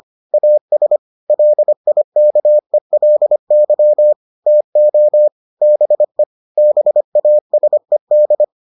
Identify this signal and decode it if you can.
Morse code, heard as a keyed tone, about 25 wpm, 600 Hz. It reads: AS LIKELY TO BE BASED